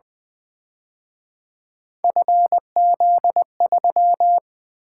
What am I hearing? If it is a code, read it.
Morse code, heard as a keyed tone, about 25 wpm, 700 Hz. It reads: FZ3
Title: Speed-Racing - Three Letter-Number Combinations - 20wpm